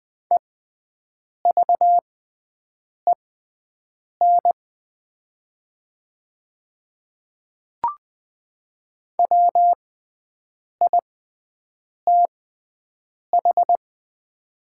Morse code, heard as a keyed tone, about 20 wpm, spaced as for 7 wpm, 700 Hz. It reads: EVEN WITH
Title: 6x Character Spacing - Top 200 Words - 20wpm